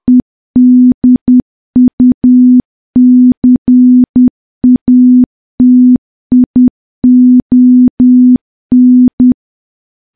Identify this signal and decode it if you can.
Morse code, heard as a keyed tone, about 10 wpm, 250 Hz. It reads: EDUCATION
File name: morse / morse-training-10WPM